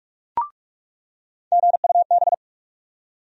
Morse code, heard as a keyed tone, about 46 wpm, 700 Hz. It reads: GUB
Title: USA Call Sign Postfixes - 3 Letters 45wpm